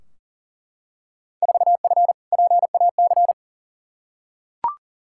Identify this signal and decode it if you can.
Morse code, heard as a keyed tone, about 40 wpm, 700 Hz. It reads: 4F PAC
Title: Field Day Exchanges Encore - 40wpm